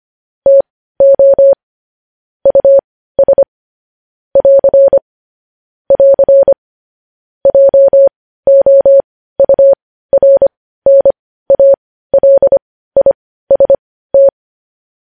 Morse code, heard as a keyed tone, about 25 wpm, spaced as for 15 wpm, 550 Hz. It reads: TO US <AR> <AR> JOURNALIST
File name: podcast / News headlines-1x-25x15